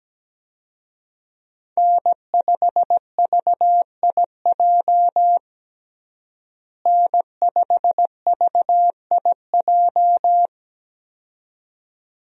Morse code, heard as a keyed tone, about 17 wpm, 700 Hz. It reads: N5VIJ N5VIJ
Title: USA Call Signs - 5 Characters - Repeated 2x - 17wpm